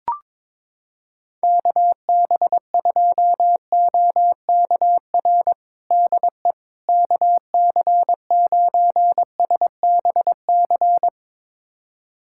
Morse code, heard as a keyed tone, about 22 wpm, 700 Hz. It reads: KB2OKR DE KC9SBC
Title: USA Callsign DE Callsign - 6 Characters - 22wpm